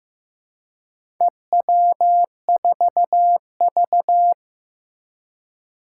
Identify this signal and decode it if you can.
Morse code, heard as a keyed tone, about 15 wpm, 700 Hz. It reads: EW4V
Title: Four Challenging Letter-Number Combinations 15wpm